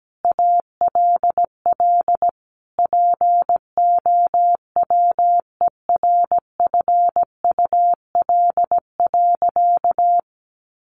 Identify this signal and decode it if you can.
Morse code, heard as a keyed tone, about 17 wpm, 700 Hz. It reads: ALL POWERFUL.
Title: Sentences from Top 1300 Words 17wpm